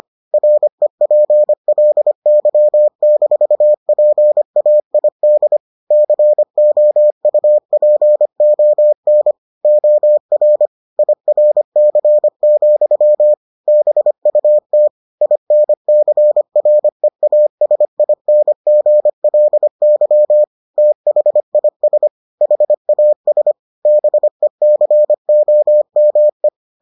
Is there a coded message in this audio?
Yes: Morse code from 0.3 to 26.5 s, heard as a keyed tone, about 25 wpm, 600 Hz. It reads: REPLY-PAID COUPON OR IRC, BUT INCREASINGLY THIS HAS BECOME